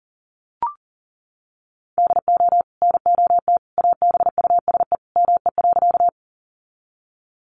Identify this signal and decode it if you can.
Morse code, heard as a keyed tone, about 40 wpm, 700 Hz. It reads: DO NOT ABUSE ME.
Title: Sentences from Top 1300 Words 40wpm